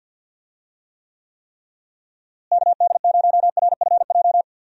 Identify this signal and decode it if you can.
Morse code, heard as a keyed tone, about 50 wpm, 700 Hz. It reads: KD0RFJ
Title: USA Call Signs - 6 Characters - Repeated 3x - 50wpm